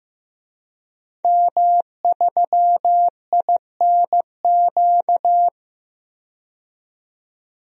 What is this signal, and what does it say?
Morse code, heard as a keyed tone, about 15 wpm, 700 Hz. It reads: M3INQ
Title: DXCC-100 Call Signs - 5 Characters - 15wpm